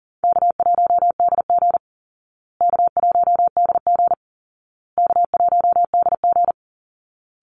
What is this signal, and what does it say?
Morse code, heard as a keyed tone, about 40 wpm, 700 Hz. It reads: K1DG K1DG K1DG
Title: Call Signs Repeated 2x - 40wpm